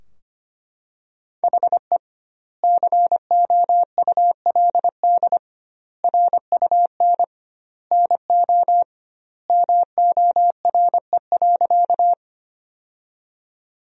Morse code, heard as a keyed tone, about 25 wpm, 700 Hz. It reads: HE COULD RUN NO MORE.
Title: 2x Word-Spacing - Sentences from Top 300 Words - 25wpm